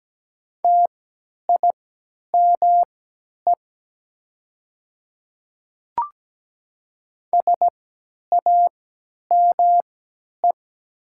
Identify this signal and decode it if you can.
Morse code, heard as a keyed tone, about 17 wpm, spaced as for 10 wpm, 700 Hz. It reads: TIME SAME
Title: 3x Character Spacing - Top 200 Words - 17wpm